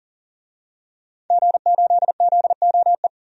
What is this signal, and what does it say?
Morse code, heard as a keyed tone, about 40 wpm, 700 Hz. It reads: G8ZOE
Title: DXCC-100 Call Signs - 5 Characters - Repeated 2x - 40wpm